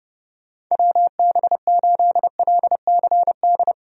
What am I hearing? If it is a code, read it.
Morse code, heard as a keyed tone, about 30 wpm, 700 Hz. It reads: WB8LCD
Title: USA Call Signs - 6 Characters - Repeated 2x - 30wpm